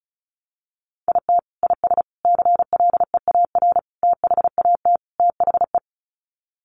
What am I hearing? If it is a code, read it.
Morse code, heard as a keyed tone, about 34 wpm, 700 Hz. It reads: IT IS CLEAR THAT THE
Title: Sets of 5 Words Encore 35wpm